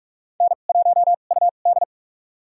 Morse code, heard as a keyed tone, about 44 wpm, spaced as for 32 wpm, 700 Hz. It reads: N1UD
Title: USA Call Signs - 4 Characters - 2x Character Spacing - 45wpm